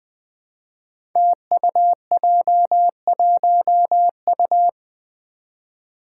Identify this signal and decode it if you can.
Morse code, heard as a keyed tone, about 20 wpm, 700 Hz. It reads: TUJ1U
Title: Five Challenging Letter-Number Combinations 20wpm